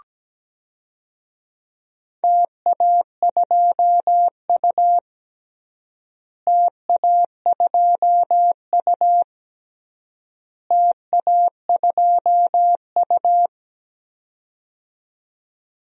Morse code, heard as a keyed tone, about 17 wpm, 700 Hz. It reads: TA2U TA2U TA2U
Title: DXCC-100 Call Signs - 4 Characters - Repeated 3x - 17wpm